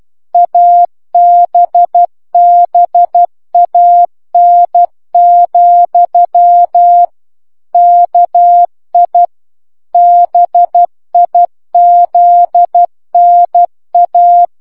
Morse code, heard as a keyed tone, about 12 wpm, 700 Hz. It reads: ABBAN, KI BIZNA